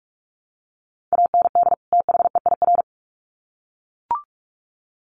Morse code, heard as a keyed tone, about 44 wpm, 700 Hz. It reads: AND THEIR
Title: Sets of 2 Words 45wpm